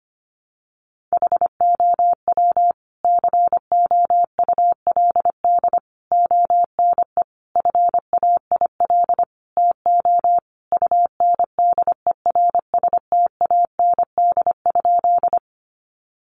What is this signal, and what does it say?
Morse code, heard as a keyed tone, about 25 wpm, 700 Hz. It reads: HOW COULD ONE FAIL TO UNDERSTAND?
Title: Sentences from Top 1100 Words 25wpm